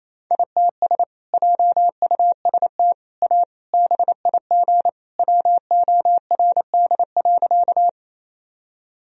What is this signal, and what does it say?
Morse code, heard as a keyed tone, about 28 wpm, 700 Hz. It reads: ITS JUST A BIG WORD.
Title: Sentences from Top 1000 Words - Review Part 1 - 28wpm